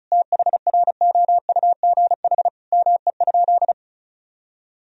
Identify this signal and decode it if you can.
Morse code, heard as a keyed tone, about 34 wpm, 700 Hz. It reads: THROUGH ME?
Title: Sentences from Top 1000 Words - Review Part 1 - 35wpm